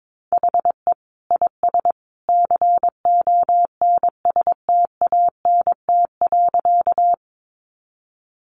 Morse code, heard as a keyed tone, about 22 wpm, 700 Hz. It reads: HE IS CONSTANT.
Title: Sentences from Top 1500 Words 22wpm